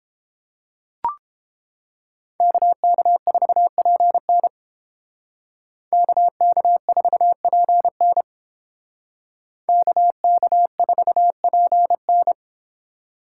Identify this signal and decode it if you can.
Morse code, heard as a keyed tone, about 28 wpm, 700 Hz. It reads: KK4PN KK4PN KK4PN